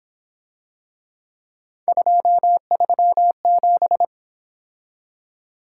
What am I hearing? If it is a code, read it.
Morse code, heard as a keyed tone, about 26 wpm, 700 Hz. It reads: 237
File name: Speed-Racing - Three Letter-Number Combinations - 22wpm